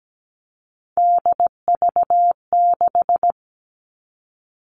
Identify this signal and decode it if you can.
Morse code, heard as a keyed tone, about 17 wpm, 700 Hz. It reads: DV6
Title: All Cut Numbers - 3-Digit 17wpm